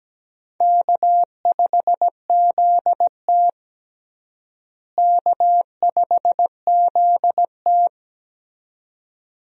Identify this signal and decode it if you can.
Morse code, heard as a keyed tone, about 17 wpm, 700 Hz. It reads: K5ZT K5ZT
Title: USA Call Signs - 4 Characters - Repeated 2x - 17wpm